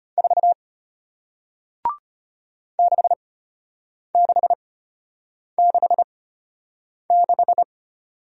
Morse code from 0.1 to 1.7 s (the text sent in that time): 4